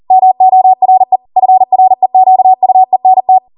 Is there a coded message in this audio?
Yes: Morse code from 0.1 to 3.4 s, heard as a keyed tone, about 40 wpm, 750 Hz. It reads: MORE FREQUENT